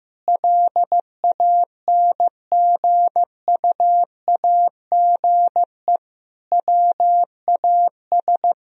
Morse code, heard as a keyed tone, about 16 wpm, 700 Hz. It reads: LANGUAGE WAS
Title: Sentences from Top 1000 Words - Review Part 4 - 15wpm